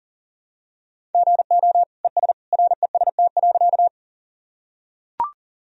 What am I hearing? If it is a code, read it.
Morse code, heard as a keyed tone, about 40 wpm, 700 Hz. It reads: GO ES REST.
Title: Sentences from Top 500 Words and Abbreviations 40wpm